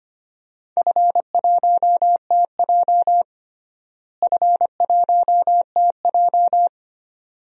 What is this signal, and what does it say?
Morse code, heard as a keyed tone, about 25 wpm, 700 Hz. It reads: F1TJ F1TJ
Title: DXCC-100 Call Signs - 4 Characters - Repeated 2x - 25wpm